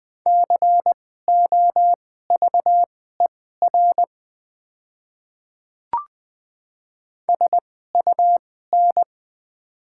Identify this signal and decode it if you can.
Morse code, heard as a keyed tone, about 20 wpm, spaced as for 14 wpm, 700 Hz. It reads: COVER SUN
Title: 2x Character Spacing - Top 300 Words - 20wpm